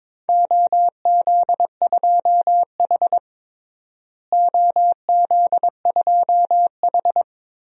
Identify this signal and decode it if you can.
Morse code, heard as a keyed tone, about 22 wpm, 700 Hz. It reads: OZ2H OZ2H
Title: DXCC-100 Call Signs - 4 Characters - Repeated 3x - 22wpm